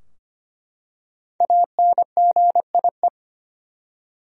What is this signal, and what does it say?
Morse code, heard as a keyed tone, about 25 wpm, 700 Hz. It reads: ANGIE
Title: USA Common Names - 25wpm